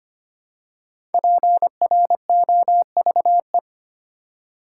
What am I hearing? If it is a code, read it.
Morse code, heard as a keyed tone, about 26 wpm, 700 Hz. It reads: PROVE
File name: 5 Letter Words - First 3 Letters Given - 25wpm